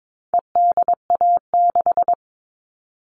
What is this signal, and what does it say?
Morse code, heard as a keyed tone, about 22 wpm, 700 Hz. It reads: EDA6
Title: All Cut Numbers - 4-Digit 22wpm